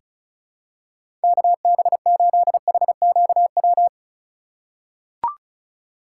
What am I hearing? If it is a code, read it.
Morse code, heard as a keyed tone, about 36 wpm, 700 Hz. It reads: KB8HQW
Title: USA Call Signs - 6 Characters - 3x Character Spacing - 35wpm